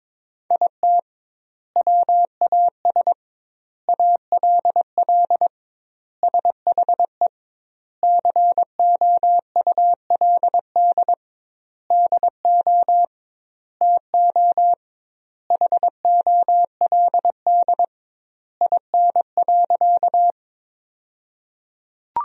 Morse code, heard as a keyed tone, about 22 wpm, 700 Hz. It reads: IT WAS ALL SHE COULD DO TO HOLD IN.
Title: 2x Word-Spacing - Sentences from Top 400 Words - 22wpm